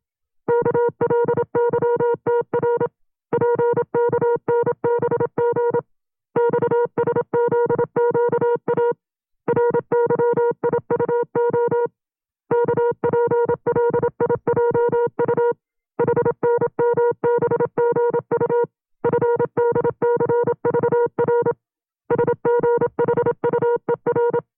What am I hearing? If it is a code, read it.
Morse code, heard as a keyed tone, about 27 wpm, 480 Hz. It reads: KLYTR PKNBG XSZQA RYIUO KPLIJU HNMBGU FDCVR SGHUER
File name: FreeFiles / CW-practice / 25WPM-side2